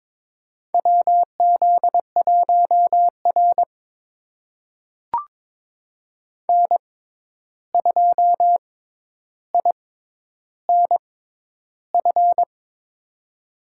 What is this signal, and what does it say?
Morse code, heard as a keyed tone, about 22 wpm, spaced as for 11 wpm, 700 Hz. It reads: WZ1R N2INF
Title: USA Call Signs - 6x Character Spacing - 22wpm